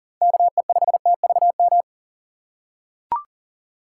Morse code, heard as a keyed tone, about 40 wpm, 700 Hz. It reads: KE5TVM
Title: USA Call Signs - 6 Characters - 4x Character Spacing - 40wpm